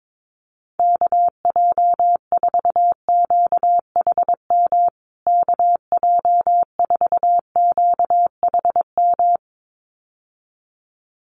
Morse code, heard as a keyed tone, about 22 wpm, 700 Hz. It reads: KJ4QHM KJ4QHM
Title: USA Call Signs Repeated 2x 22wpm